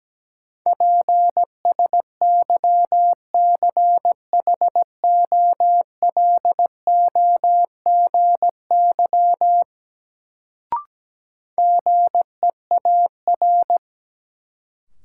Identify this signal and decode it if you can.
Morse code, heard as a keyed tone, about 17 wpm, 700 Hz. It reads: PSYCHOLOGY GEAR
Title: Sentences from Top 1400 Words - New Word Drill - 17wpm